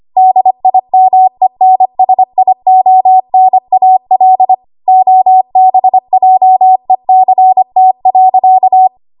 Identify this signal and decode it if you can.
Morse code, heard as a keyed tone, about 25 wpm, 750 Hz. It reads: DIMENSIONAL OBJECT.
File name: Archive / 25 WPM / 25WPM